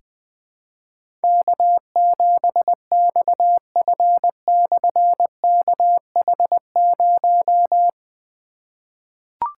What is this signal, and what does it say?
Morse code, heard as a keyed tone, about 20 wpm, 700 Hz. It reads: K7XF/KH0
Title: Difficult Call Signs 20wpm